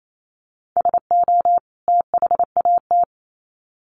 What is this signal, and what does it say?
Morse code, heard as a keyed tone, about 28 wpm, 700 Hz. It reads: SO THAT
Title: Sets of 2 Words 28wpm